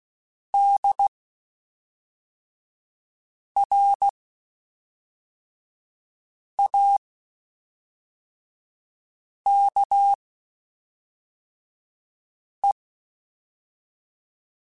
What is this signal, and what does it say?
Morse code, heard as a keyed tone, about 16 wpm, spaced as for 3 wpm, 800 Hz. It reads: DRAKE
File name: (spaced) / QSOs Spaced-mp3s / WD0FBT 1 @5 sp